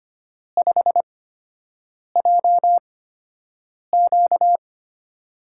Morse code, heard as a keyed tone, about 25 wpm, spaced as for 7 wpm, 700 Hz. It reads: 5JQ
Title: USA Call Signs - 4 Characters - 8x Character Spacing - 25wpm